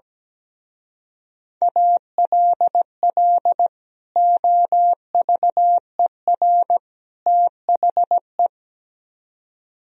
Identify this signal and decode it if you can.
Morse code, heard as a keyed tone, about 17 wpm, 700 Hz. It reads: ALL OVER THE
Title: Speed-Racing - Sets of 3 Words 17wpm